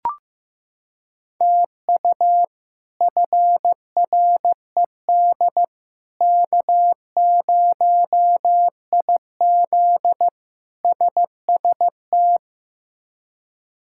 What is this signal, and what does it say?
Morse code, heard as a keyed tone, about 16 wpm, 700 Hz. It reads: TU FRED K0IZ SST
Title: K1USN SST QSO Practice - 15wpm